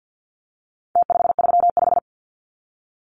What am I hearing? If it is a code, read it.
Morse code, heard as a keyed tone, about 50 wpm, 700 Hz. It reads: T535
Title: Numbers - Contest Style - 4-Digit 50wpm